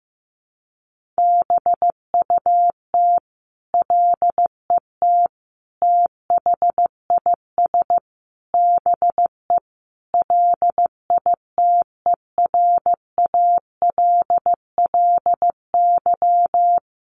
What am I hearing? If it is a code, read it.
Morse code, heard as a keyed tone, about 15 wpm, 700 Hz. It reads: BUT LET THIS BE LITERALLY
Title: Sentences from Top 1400 Words 15wpm